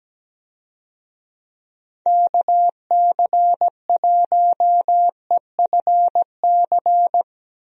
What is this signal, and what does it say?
Morse code, heard as a keyed tone, about 17 wpm, 700 Hz. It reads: KC1EFC
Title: USA Call Signs - 6 Characters - Repeated 3x - 17wpm